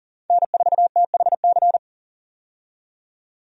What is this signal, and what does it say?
Morse code, heard as a keyed tone, about 40 wpm, 700 Hz. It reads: N4THC
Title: USA Call Signs - 5 Characters - 40wpm